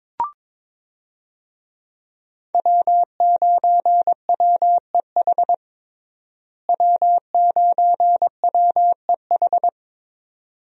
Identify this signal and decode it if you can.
Morse code, heard as a keyed tone, about 21 wpm, 700 Hz. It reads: W9WEH W9WEH
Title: USA Call Signs - 5 Characters - Repeated 2x - 22wpm